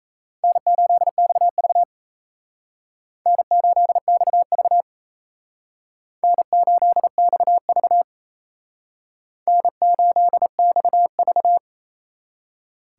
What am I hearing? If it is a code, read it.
Morse code, heard as a keyed tone, about 35 wpm, 700 Hz. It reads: N8XV N8XV N8XV N8XV